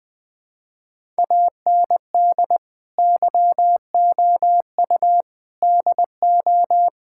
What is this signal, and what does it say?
Morse code, heard as a keyed tone, about 20 wpm, 700 Hz. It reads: AND YOU DO
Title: Sentences with 4-Word Phrases - Spoken Beginning - 20wpm